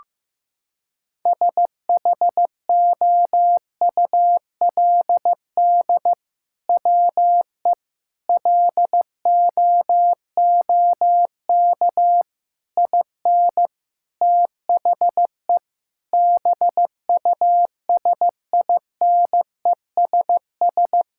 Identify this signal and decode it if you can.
Morse code, heard as a keyed tone, about 15 wpm, 700 Hz. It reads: SHOULD WE LOOK IN THE BUSINESS
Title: Sentences from Top 1600 Words 15wpm